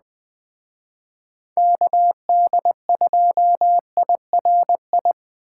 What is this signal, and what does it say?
Morse code, heard as a keyed tone, about 20 wpm, 700 Hz. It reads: KD2IRI
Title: Speed Racing - USA 2x3 Call Signs 17wpm